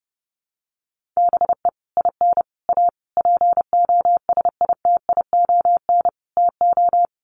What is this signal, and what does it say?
Morse code, heard as a keyed tone, about 30 wpm, 700 Hz. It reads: BE IN A POSITION TO
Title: Sets of 5 Words Encore2 30wpm